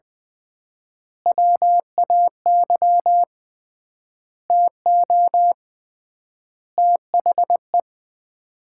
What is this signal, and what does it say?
Morse code, heard as a keyed tone, about 20 wpm, 700 Hz. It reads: WAY TO THE
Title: 3x Word-Spacing - Sets of 5 Words - 20wpm